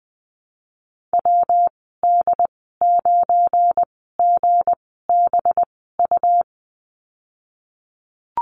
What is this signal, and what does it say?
Morse code, heard as a keyed tone, about 20 wpm, spaced as for 15 wpm, 700 Hz. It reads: WD9GBU
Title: 186 Lesson 30g - Letter b - Call Signs 20wpm